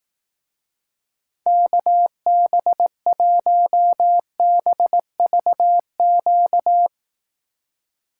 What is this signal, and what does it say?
Morse code, heard as a keyed tone, about 18 wpm, 700 Hz. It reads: KB1BVQ
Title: Speed Racing - USA 2x3 Call Signs 15wpm